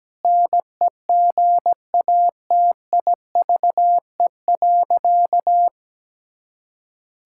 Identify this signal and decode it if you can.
Morse code, heard as a keyed tone, about 17 wpm, 700 Hz. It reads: NEGATIVE.
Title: Sentences from Top 1000 Words - Review Part 6 - 17wpm